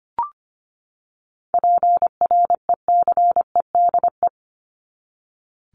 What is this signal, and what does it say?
Morse code, heard as a keyed tone, about 25 wpm, 700 Hz. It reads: PRECEDE
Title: ING Suffix - Top 100 Words with Base Words 25wpm